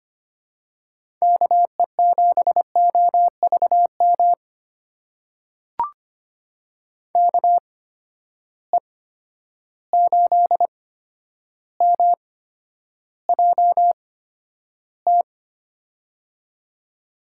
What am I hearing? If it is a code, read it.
Morse code, heard as a keyed tone, about 25 wpm, spaced as for 11 wpm, 700 Hz. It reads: KE7OVM KE8MJT
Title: USA Call Signs - 6 Characters - 8x Character Spacing - 25wpm